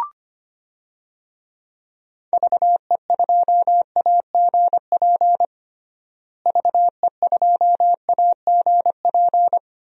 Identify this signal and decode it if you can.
Morse code, heard as a keyed tone, about 25 wpm, 700 Hz. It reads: VE2AGP VE2AGP
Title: DXCC-100 Call Signs - 6 Characters - Repeated 2x - 25wpm